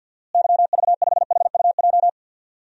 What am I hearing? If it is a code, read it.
Morse code, heard as a keyed tone, about 52 wpm, 700 Hz. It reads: YV5HUJ
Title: DXCC-100 Call Signs - 6 Characters - Repeated 2x - 50wpm